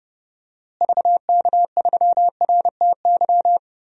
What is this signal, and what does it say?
Morse code, heard as a keyed tone, about 30 wpm, 700 Hz. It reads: VK3RTY
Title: DXCC-100 Call Signs - 6 Characters - Repeated 3x - 30wpm